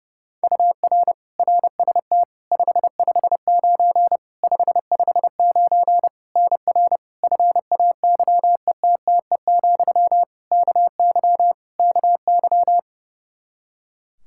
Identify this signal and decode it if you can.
Morse code, heard as a keyed tone, about 30 wpm, 700 Hz. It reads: UR RST 559 559 NR FAYETTE, KY KY
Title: Practice QSO - Exchange 1 - Beginning - 30wpm